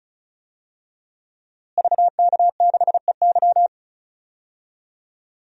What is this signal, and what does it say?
Morse code, heard as a keyed tone, about 36 wpm, 700 Hz. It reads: VK6EY
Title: DXCC-100 Call Signs - 5 Characters - Repeated 2x - 35wpm